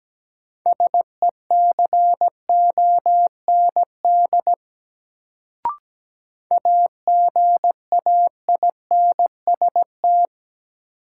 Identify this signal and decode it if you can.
Morse code, heard as a keyed tone, about 17 wpm, 700 Hz. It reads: SECOND AGAINST
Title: Sets of 2 Words - New Word Drill - 17wpm